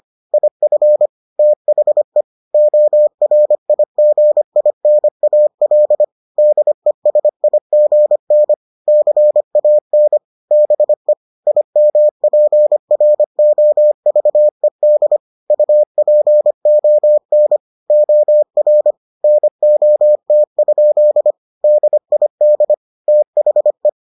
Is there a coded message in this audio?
Yes: Morse code from 0.3 to 23.9 s, heard as a keyed tone, about 25 wpm, 600 Hz. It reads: IF THE ORIGINAL DESIGN CAN BE IMPROVED UPON OR NOT? DID THE